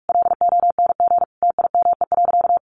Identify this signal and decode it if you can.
Morse code, heard as a keyed tone, about 44 wpm, 700 Hz. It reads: LONG TIME.
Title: Sentences from Top 400 Words 45wpm